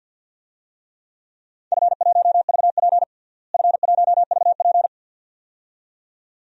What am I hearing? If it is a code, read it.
Morse code, heard as a keyed tone, about 52 wpm, 700 Hz. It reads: F1VP F1VP
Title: DXCC-100 Call Signs - 4 Characters - Repeated 2x - 50wpm